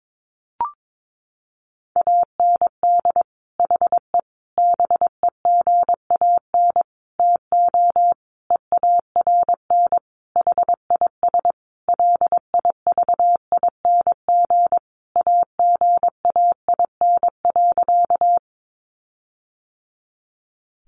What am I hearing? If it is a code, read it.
Morse code, heard as a keyed tone, about 22 wpm, 700 Hz. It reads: AND HE BEGAN TO EARN HIS LIVING AGAIN.
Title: Sentences from Top 1500 Words 22wpm